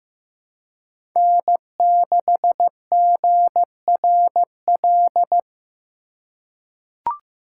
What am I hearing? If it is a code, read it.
Morse code, heard as a keyed tone, about 15 wpm, 700 Hz. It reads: N6GRL
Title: USA Call Signs - 5 Characters - 4x Character Spacing - 15wpm